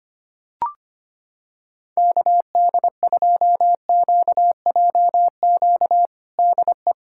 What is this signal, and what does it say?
Morse code, heard as a keyed tone, about 25 wpm, 700 Hz. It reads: KD2QJQ DE
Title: USA Callsign DE Callsign 25wpm